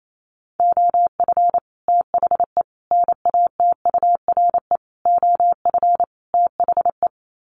Morse code, heard as a keyed tone, about 28 wpm, 700 Hz. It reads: OF THE NATURE OF THE
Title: Sets of 5 Words Encore2 28wpm